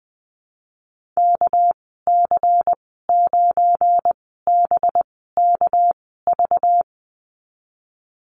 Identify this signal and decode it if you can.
Morse code, heard as a keyed tone, about 19 wpm, spaced as for 14 wpm, 700 Hz. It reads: KC9BKV
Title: 231 Lesson 37g - Letter x - Call Signs 20wpm